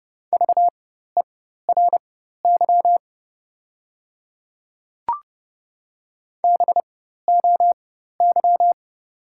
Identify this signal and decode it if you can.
Morse code, heard as a keyed tone, about 30 wpm, spaced as for 14 wpm, 700 Hz. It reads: VERY BOY
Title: 4x Character Spacing - Top 200 Words - 30wpm